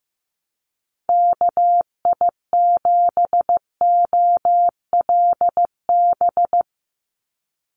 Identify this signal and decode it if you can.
Morse code, heard as a keyed tone, about 15 wpm, 700 Hz. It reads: KI7OLB
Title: USA Call Signs 15wpm